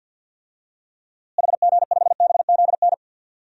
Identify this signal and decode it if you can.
Morse code, heard as a keyed tone, about 51 wpm, 700 Hz. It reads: HG5BZN
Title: DXCC-100 Call Signs - 6 Characters - Repeated 3x - 50wpm